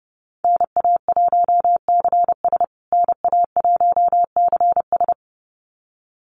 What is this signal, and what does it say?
Morse code, heard as a keyed tone, about 28 wpm, 700 Hz. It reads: NA1CS NA1CS
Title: USA Call Signs Repeated 2x 30wpm